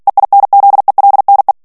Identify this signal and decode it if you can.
Morse code, heard as a keyed tone, about 48 wpm, 800 Hz. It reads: EINZELNE
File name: morse mp3 / ALLEMAND texte clair 2 vitesse 45 mots minute